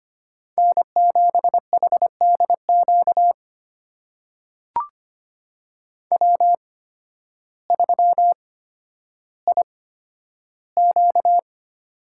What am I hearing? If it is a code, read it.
Morse code, heard as a keyed tone, about 25 wpm, spaced as for 12 wpm, 700 Hz. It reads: N7HDQ W3IQ
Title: USA Call Signs - 8x Character Spacing - 25wpm